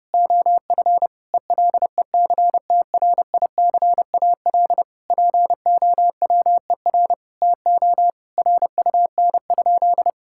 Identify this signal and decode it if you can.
Morse code, heard as a keyed tone, about 30 wpm, 700 Hz. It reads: OF ELECTRICAL POWER TO RUN?
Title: Sentences from Top 1000 Words - Review Part 6 - 30wpm